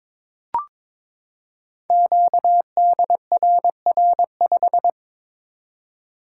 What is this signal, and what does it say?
Morse code, heard as a keyed tone, about 22 wpm, 700 Hz. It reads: QDRR5
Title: Code Groups - 5 Letter:Numbers - 22wpm